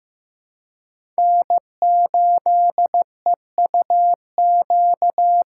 Morse code, heard as a keyed tone, about 15 wpm, 700 Hz. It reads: N8EUQ
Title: USA Call Signs - 5 Characters - 2x Character Spacing - 15wpm